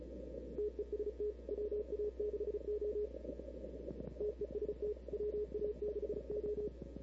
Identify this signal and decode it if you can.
Morse code, heard as a keyed tone, about 36 wpm, 420 Hz. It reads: TEST WA6O TEST WA6O
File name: CW / 2005-k5tr arrl cw dx -00:00